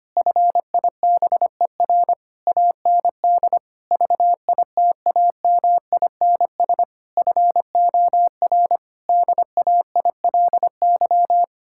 Morse code, heard as a keyed tone, about 25 wpm, 700 Hz. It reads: FIBER AND VITAMINS FOR DAILY
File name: practice / 25wpm / Difficult Sentences with Hints - 25wpm